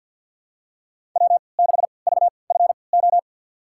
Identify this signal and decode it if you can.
Morse code, heard as a keyed tone, about 50 wpm, spaced as for 28 wpm, 700 Hz. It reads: W6VFO
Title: USA Call Signs - 5 Characters - 3x Character Spacing - 50wpm